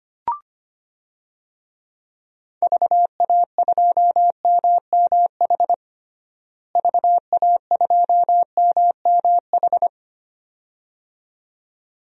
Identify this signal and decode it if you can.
Morse code, heard as a keyed tone, about 25 wpm, 700 Hz. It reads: VA2MMH VA2MMH
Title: DXCC-100 Call Signs - 6 Characters - Repeated 2x - 25wpm